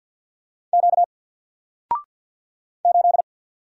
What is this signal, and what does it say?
Morse code, heard as a keyed tone, about 48 wpm, 700 Hz. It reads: Q 8
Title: Single Letter-Number Alt 50wpm